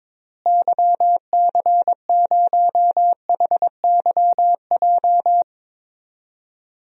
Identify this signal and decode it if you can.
Morse code, heard as a keyed tone, about 22 wpm, 700 Hz. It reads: YC0HYJ
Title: DXCC-100 Call Signs - 6 Characters - Repeated 2x - 22wpm